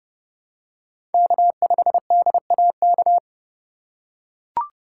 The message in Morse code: K5DAK

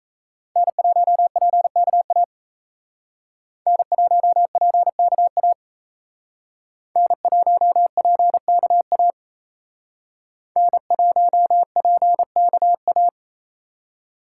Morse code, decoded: N1PKA N1PKA N1PKA N1PKA

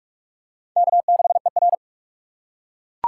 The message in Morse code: K6ER

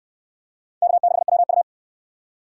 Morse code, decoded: K6WU

